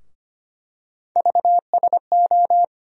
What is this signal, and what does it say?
Morse code, heard as a keyed tone, about 25 wpm, 700 Hz. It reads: VSO
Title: USA Call Sign Postfixes - 3 Letters 25wpm